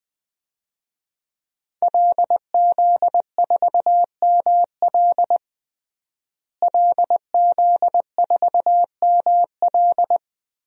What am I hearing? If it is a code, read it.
Morse code, heard as a keyed tone, about 20 wpm, 700 Hz. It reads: LZ4ML LZ4ML
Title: DXCC-100 Call Signs - 5 Characters - Repeated 2x - 20wpm